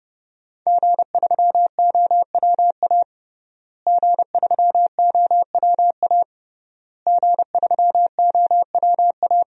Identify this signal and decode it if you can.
Morse code, heard as a keyed tone, about 32 wpm, 700 Hz. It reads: G3OWA G3OWA G3OWA
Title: DXCC-100 Call Signs - 5 Characters - Repeated 3x - 30wpm